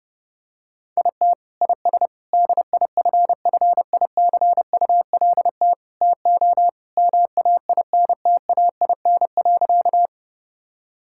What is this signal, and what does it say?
Morse code, heard as a keyed tone, about 32 wpm, 700 Hz. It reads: IT IS DIFFICULT TO MAINTAIN.
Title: Sentences from Top 1000 Words - Review Part 5 - 30wpm